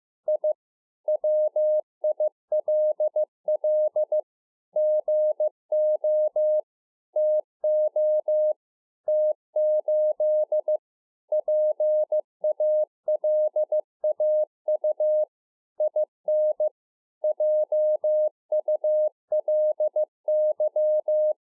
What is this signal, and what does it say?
Morse code, heard as a keyed tone, about 15 wpm, 610 Hz. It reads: I WILL GO TO T8 PALAU IN JULY